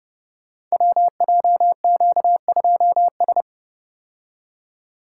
Morse code, heard as a keyed tone, about 30 wpm, 700 Hz. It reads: WJQ2S
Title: Code Groups - 5 Letter:Numbers - 30wpm